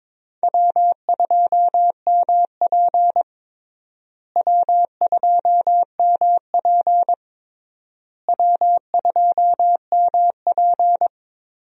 Morse code, heard as a keyed tone, about 22 wpm, 700 Hz. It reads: W2MP W2MP W2MP
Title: USA Call Signs - 4 Characters - Repeated 3x - 22wpm